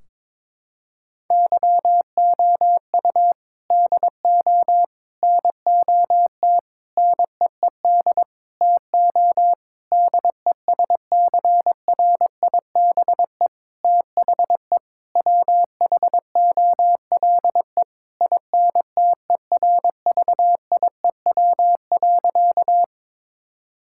Morse code, heard as a keyed tone, about 21 wpm, 700 Hz. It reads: YOU DO NOT NEED TO DESCRIBE THE WHOLE INTERVIEW.
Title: Sentences from Top 1600 Words 22wpm